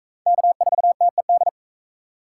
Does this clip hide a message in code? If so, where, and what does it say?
Yes: Morse code from 0.3 to 1.5 s, heard as a keyed tone, about 39 wpm, 700 Hz. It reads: K4TED